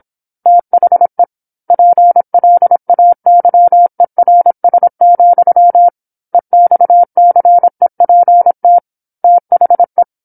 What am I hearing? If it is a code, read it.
Morse code, heard as a keyed tone, about 27 wpm, 700 Hz. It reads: THE PLAYERS, EXCEPT THE
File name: 08-Chapter-08 26 WPM